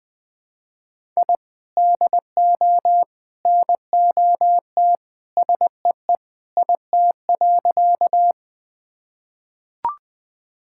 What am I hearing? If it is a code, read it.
Morse code, heard as a keyed tone, about 20 wpm, 700 Hz. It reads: I DO NOT SEE IT.
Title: Sentences with 3-Word Phrases - 20wpm